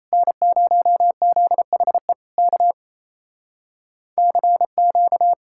Morse code, heard as a keyed tone, about 38 wpm, 700 Hz. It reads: N0ZHE K CQ